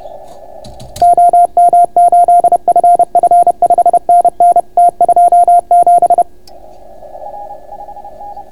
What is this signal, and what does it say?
Morse code, heard as a keyed tone, about 31 wpm, 690 Hz. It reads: OM8FF 5NN T27